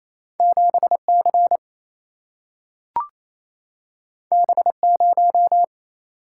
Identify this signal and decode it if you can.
Morse code, heard as a keyed tone, about 28 wpm, 700 Hz. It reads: 7 C B 0
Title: Single-Letter Single-Number 28wpm